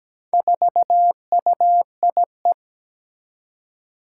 Morse code, heard as a keyed tone, about 17 wpm, 700 Hz. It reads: 4UIE
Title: Four Challenging Letter-Number Combinations 17wpm